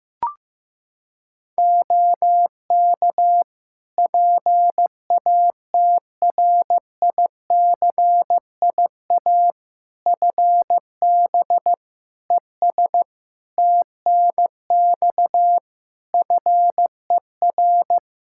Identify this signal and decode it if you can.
Morse code, heard as a keyed tone, about 16 wpm, 700 Hz. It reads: OK PATRICIA FB ES TNX FER
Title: Practice QSO - Exchange 2 - Info - 15wpm